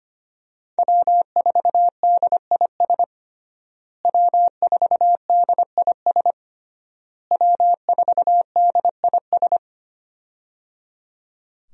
Morse code, heard as a keyed tone, about 25 wpm, 700 Hz. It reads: W4DIS W4DIS W4DIS
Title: USA Call Signs - 5 Characters - Repeated 3x - 25wpm